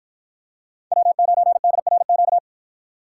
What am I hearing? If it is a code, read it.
Morse code, heard as a keyed tone, about 51 wpm, 700 Hz. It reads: W9DRQ